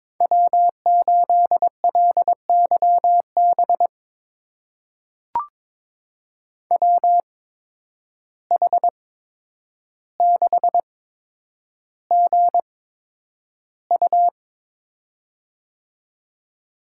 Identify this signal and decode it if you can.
Morse code, heard as a keyed tone, about 22 wpm, spaced as for 9 wpm, 700 Hz. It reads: W8LYB WH6GU
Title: USA Call Signs - 5 Characters - 8x Character Spacing - 22wpm